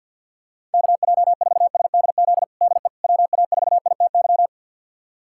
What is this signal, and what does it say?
Morse code, heard as a keyed tone, about 50 wpm, 690 Hz. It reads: KJ4SDZ DE WA4ITY
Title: USA Callsign DE Callsign - 6 Characters - 50wpm